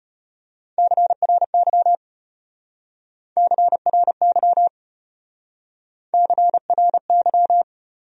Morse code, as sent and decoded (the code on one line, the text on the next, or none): -.-. .-. -.-- / -.-. .-. -.-- / -.-. .-. -.--
CRY CRY CRY